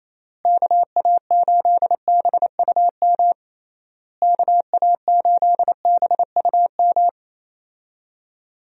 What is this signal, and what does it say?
Morse code, heard as a keyed tone, about 28 wpm, 700 Hz. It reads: KA8BUM KA8BUM
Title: USA Call Signs - 6 Characters - Repeated 3x - 28wpm